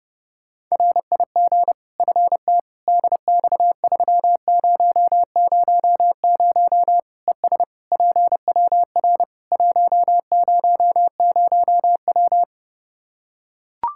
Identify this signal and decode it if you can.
Morse code, heard as a keyed tone, about 30 wpm, 700 Hz. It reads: RIG FT DX3000 ES PWR 100W
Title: Practice QSO - Exchange 2 - Info - 30wpm